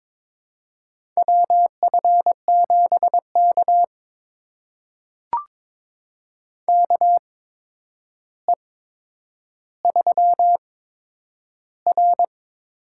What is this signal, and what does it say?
Morse code, heard as a keyed tone, about 22 wpm, spaced as for 9 wpm, 700 Hz. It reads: WF7K KE3R
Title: USA Call Signs - 4 Characters - 8x Character Spacing - 22wpm